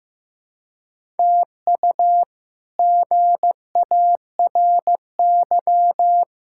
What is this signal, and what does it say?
Morse code, heard as a keyed tone, about 15 wpm, 700 Hz. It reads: TU GARY
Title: K1USN SST QSO Practice - 15wpm